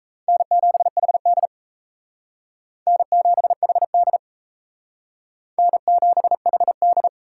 Morse code, heard as a keyed tone, about 34 wpm, 700 Hz. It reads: N7HD N7HD N7HD
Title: Speed Racing - USA 1x2 Call Signs 28wpm